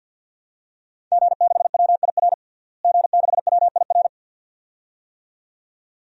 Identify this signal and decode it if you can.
Morse code, heard as a keyed tone, about 52 wpm, 700 Hz. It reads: G6WIR G6WIR
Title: DXCC-100 Call Signs - 5 Characters - Repeated 2x - 50wpm